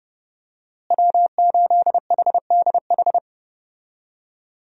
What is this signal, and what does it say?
Morse code, heard as a keyed tone, about 30 wpm, 700 Hz. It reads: W8HDH
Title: USA Call Signs - 5 Characters - Repeated 2x - 30wpm